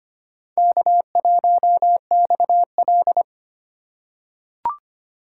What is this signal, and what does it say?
Morse code, heard as a keyed tone, about 25 wpm, 700 Hz. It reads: K1XL
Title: USA Call Signs - 4 Characters - 3x Character Spacing - 25wpm